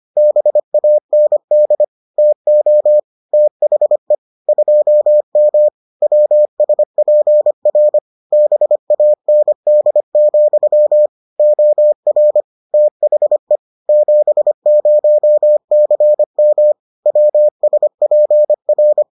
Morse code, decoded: BAND TO THE 2M WSPR BAND, OR THE 70CM WSPR

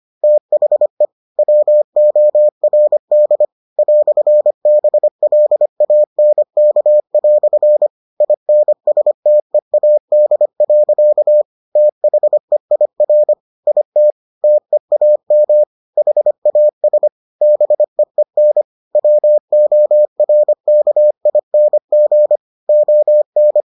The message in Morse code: THE WORD "BLANK" INSTEAD. THEIR IT TEAM HAS BEEN WORKING ON

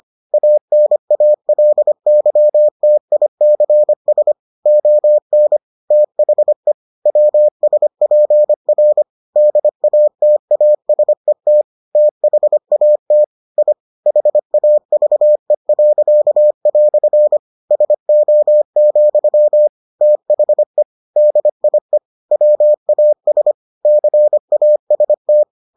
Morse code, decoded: ANALYTICS ON THE WSPR DATASET THAT I HAVE." SO, THE DIE WAS CAST

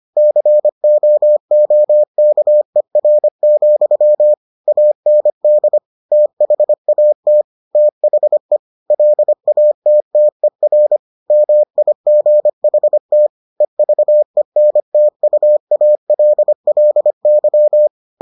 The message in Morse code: COOKER, AND THAT THE LATTER MIGHT EVENTUALLY

